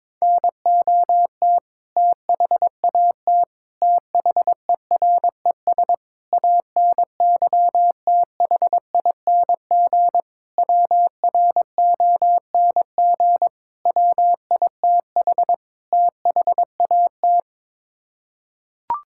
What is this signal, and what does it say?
Morse code, heard as a keyed tone, about 22 wpm, 700 Hz. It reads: NOT THAT THERES ANYTHING WRONG WITH THAT
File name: practice / 22wpm / Sets of 7 Words - 22wpm